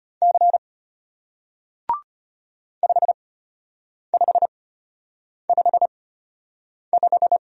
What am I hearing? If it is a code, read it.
Morse code, heard as a keyed tone, about 34 wpm, spaced as for 8 wpm, 700 Hz. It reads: C 5555